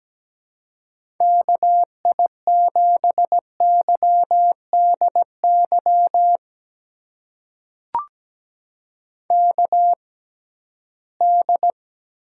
Morse code, heard as a keyed tone, about 17 wpm, spaced as for 13 wpm, 700 Hz. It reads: KI7YDY KD
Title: USA Call Signs - 6 Characters - 6x Character Spacing - 17wpm